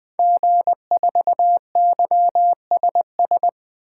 Morse code, heard as a keyed tone, about 20 wpm, 700 Hz. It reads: G4YSS
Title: 2023 SOTA Activator Call Signs - 20wpm